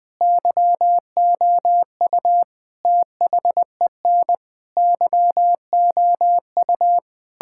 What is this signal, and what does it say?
Morse code, heard as a keyed tone, about 20 wpm, 700 Hz. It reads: YOU THEN YOU
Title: Sentences with 8-Word Phrases - 20wpm